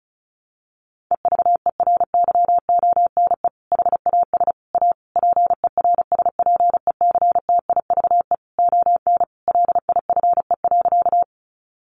Morse code, heard as a keyed tone, about 34 wpm, 700 Hz. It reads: EVERYONE HAS A PERSPECTIVE ON LIFE.
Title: Sentences from Top 1200 Words 35wpm